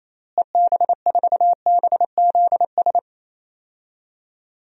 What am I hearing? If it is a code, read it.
Morse code, heard as a keyed tone, about 28 wpm, 700 Hz. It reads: EB4BZS
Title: DXCC-100 Call Signs - 6 Characters - Repeated 3x - 28wpm